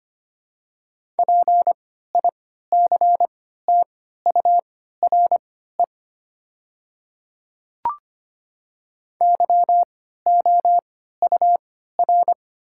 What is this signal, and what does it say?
Morse code, heard as a keyed tone, about 25 wpm, spaced as for 14 wpm, 700 Hz. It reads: PICTURE YOUR
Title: 3x Character Spacing - Top 300 Words - 25wpm